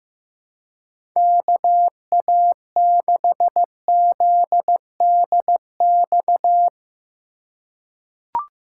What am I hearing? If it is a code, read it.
Morse code, heard as a keyed tone, about 15 wpm, 700 Hz. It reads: KA6ZDX
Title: USA Call Signs - 2x Character Spacing - 15wpm